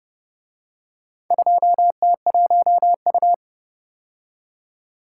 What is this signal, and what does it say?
Morse code, heard as a keyed tone, about 29 wpm, 700 Hz. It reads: 2T1U